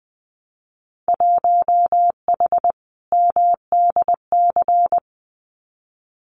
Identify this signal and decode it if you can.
Morse code, heard as a keyed tone, about 20 wpm, 700 Hz. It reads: 1H MDC
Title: Winter Field Day Exchanges 20wpm